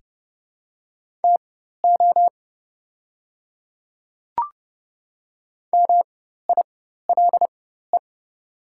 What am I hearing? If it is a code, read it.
Morse code, heard as a keyed tone, about 30 wpm, spaced as for 14 wpm, 700 Hz. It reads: TO MILE
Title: 4x Character Spacing - Top 300 Words - 30wpm